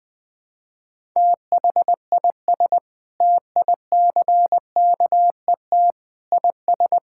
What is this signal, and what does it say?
Morse code, heard as a keyed tone, about 20 wpm, 700 Hz. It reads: THIS TICKET IS
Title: Sentences with 7-Word Phrases - 20wpm